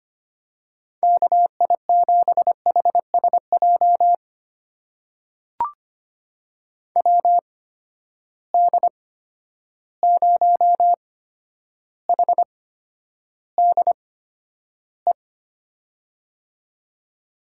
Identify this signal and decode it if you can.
Morse code, heard as a keyed tone, about 25 wpm, spaced as for 11 wpm, 700 Hz. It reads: KI7HSJ WD0HDE
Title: USA Call Signs - 6 Characters - 8x Character Spacing - 25wpm